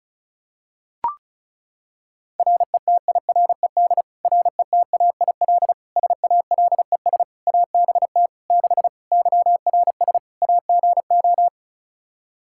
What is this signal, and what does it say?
Morse code, heard as a keyed tone, about 36 wpm, 700 Hz. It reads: RETIRED RETAIL SALES ABT 6 YRS AGO
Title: Practice QSO - Exchange 3 - More Info - 35wpm